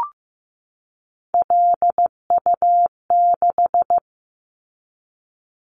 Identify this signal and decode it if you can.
Morse code, heard as a keyed tone, about 15 wpm, 700 Hz. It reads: LU6
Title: Callsign Prefixes for Top 50 DXCC - 15wpm